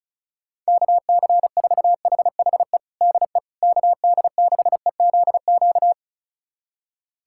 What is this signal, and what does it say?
Morse code, heard as a keyed tone, about 36 wpm, 690 Hz. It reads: KC4HHE DE KD6EZQ
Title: USA Callsign DE Callsign - 6 Characters - 35wpm